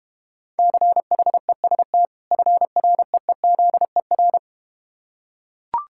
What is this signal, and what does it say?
Morse code, heard as a keyed tone, about 31 wpm, 700 Hz. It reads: CHEST FREEZER